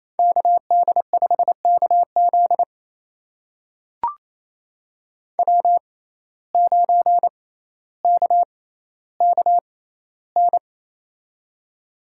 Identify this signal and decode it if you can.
Morse code, heard as a keyed tone, about 28 wpm, spaced as for 14 wpm, 700 Hz. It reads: KD5KZ W9KKN
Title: USA Call Signs - 5 Characters - 6x Character Spacing - 28wpm